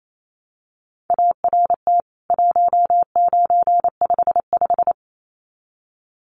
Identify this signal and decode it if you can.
Morse code, heard as a keyed tone, about 28 wpm, 700 Hz. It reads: ART 1955
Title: CWT Exchange 28wpm